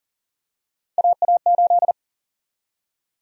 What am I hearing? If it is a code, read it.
Morse code, heard as a keyed tone, about 40 wpm, 700 Hz. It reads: AA8
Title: USA Call Sign Prefixes - 40wpm